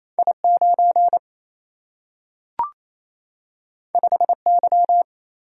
Morse code, heard as a keyed tone, about 28 wpm, 700 Hz. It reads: I 9 5 Y
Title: Single-Letter Single-Number 28wpm